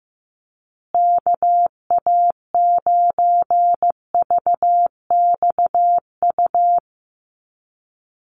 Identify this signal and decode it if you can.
Morse code, heard as a keyed tone, about 15 wpm, 700 Hz. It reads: KA9VXU
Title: USA Call Signs Repeated 2x 15wpm